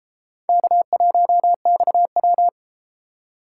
Morse code, heard as a keyed tone, about 30 wpm, 700 Hz. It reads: K1XW